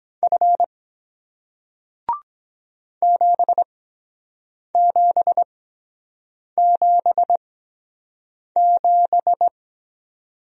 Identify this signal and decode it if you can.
Morse code, heard as a keyed tone, about 23 wpm, spaced as for 7 wpm, 700 Hz. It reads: F 7777